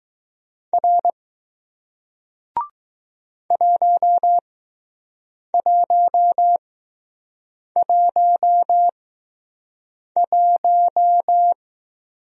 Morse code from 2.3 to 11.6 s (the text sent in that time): 1111